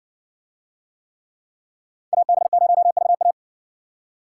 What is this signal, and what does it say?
Morse code, heard as a keyed tone, about 60 wpm, 700 Hz. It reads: AB0VA